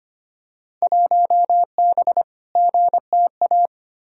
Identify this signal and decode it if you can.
Morse code, heard as a keyed tone, about 25 wpm, 700 Hz. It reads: 1B GTA
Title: Field Day Exchanges Encore - 25wpm